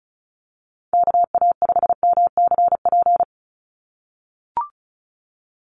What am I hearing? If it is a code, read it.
Morse code, heard as a keyed tone, about 34 wpm, 700 Hz. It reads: KA5MCP
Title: USA Call Signs 35wpm